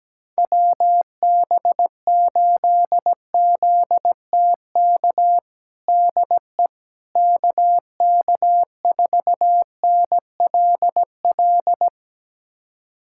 Morse code, heard as a keyed tone, about 17 wpm, 700 Hz. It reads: WB8ZTK DE KK4NLL
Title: USA Callsign DE Callsign - 6 Characters - 17wpm